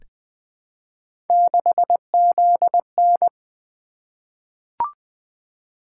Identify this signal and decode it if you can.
Morse code, heard as a keyed tone, about 20 wpm, 700 Hz. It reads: 6ZN
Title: Three Letter-Number Combinations - 20wpm